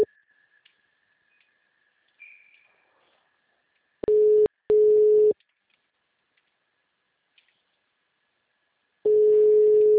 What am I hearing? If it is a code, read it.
Ringback tone (Europe, 425 Hz).